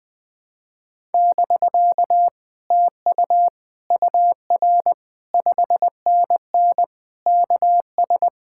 Morse code, heard as a keyed tone, about 20 wpm, 700 Hz. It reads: <BK> TU UR 5NN KS
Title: Speed-Racing Common QSO Phrases Encore - 20wpm